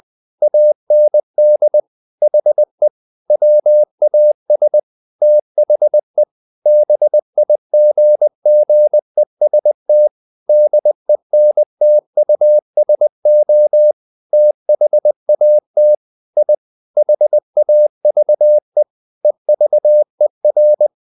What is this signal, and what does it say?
Morse code, heard as a keyed tone, about 20 wpm, 600 Hz. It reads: AND HE WAS THE BIGGEST DENTUSO THAT I HAVE EVER
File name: the old man and the sea part 2 20wpm